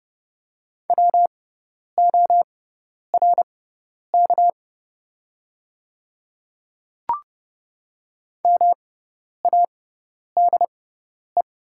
Morse code, heard as a keyed tone, about 30 wpm, spaced as for 10 wpm, 700 Hz. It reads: WORK MADE
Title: 6x Character Spacing - Top 300 Words - 30wpm